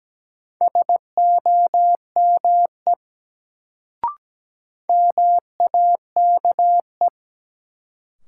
Morse code, heard as a keyed tone, about 17 wpm, 700 Hz. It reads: SOME MAKE
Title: Sentences from Top 100 Words - New Word Drill - 17wpm